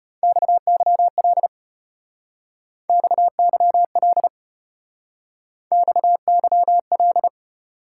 Morse code, heard as a keyed tone, about 34 wpm, 700 Hz. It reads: XYL XYL XYL